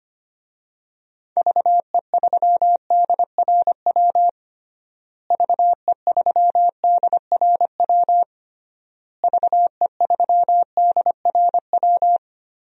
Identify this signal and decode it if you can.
Morse code, heard as a keyed tone, about 25 wpm, 700 Hz. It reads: VE3DRW VE3DRW VE3DRW
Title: DXCC-100 Call Signs - 6 Characters - Repeated 3x - 25wpm